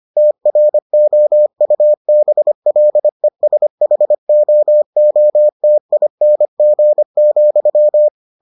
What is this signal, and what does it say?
Morse code, heard as a keyed tone, about 25 wpm, 600 Hz. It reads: TROUBLESHOOTING,